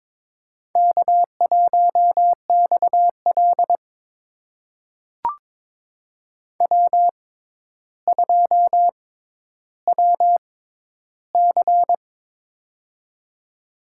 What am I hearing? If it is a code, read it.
Morse code, heard as a keyed tone, about 22 wpm, spaced as for 11 wpm, 700 Hz. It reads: K1XL W2WC
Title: USA Call Signs - 4 Characters - 6x Character Spacing - 22wpm